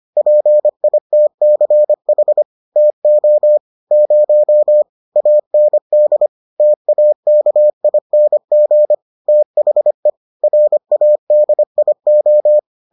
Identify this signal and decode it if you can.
Morse code, heard as a keyed tone, about 25 wpm, 600 Hz. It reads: PITCH TO 0 AND TAKING THE RADIO